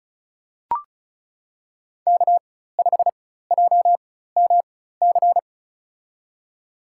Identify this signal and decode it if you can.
Morse code, heard as a keyed tone, about 36 wpm, spaced as for 16 wpm, 700 Hz. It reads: K5JMC
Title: USA Call Signs - 4x Character Spacing - 35wpm